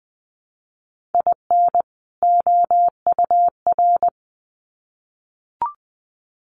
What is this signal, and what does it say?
Morse code, heard as a keyed tone, about 20 wpm, 700 Hz. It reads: IN OUR
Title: Sets of 2 Words 20wpm